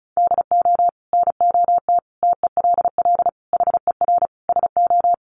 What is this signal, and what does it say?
Morse code, heard as a keyed tone, about 34 wpm, 700 Hz. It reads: DO NOT TELL HER SO
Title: Sentences from Top 1500 Words 35wpm